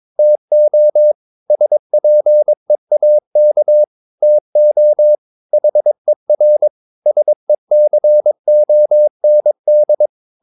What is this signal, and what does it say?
Morse code, heard as a keyed tone, about 22 wpm, 600 Hz. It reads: TO SPEAK TO HER SECOND
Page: Romanen Never af Ken Follet på engelsk.